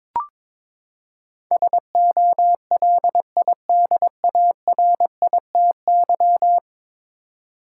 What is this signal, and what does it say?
Morse code, heard as a keyed tone, about 22 wpm, 700 Hz. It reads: SOLIDARITY
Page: morse-code-ninja-practice-22wpm